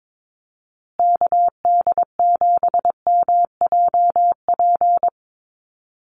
Morse code, heard as a keyed tone, about 22 wpm, 690 Hz. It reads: KD7MJP
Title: USA Call Signs Repeated 3x 22wpm